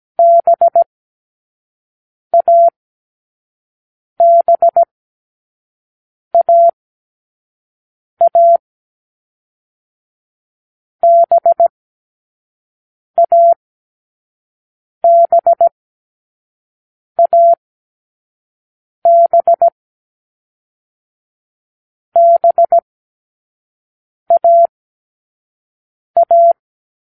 Morse code, heard as a keyed tone, about 17 wpm, 680 Hz. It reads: B A B A A B A B A B B A A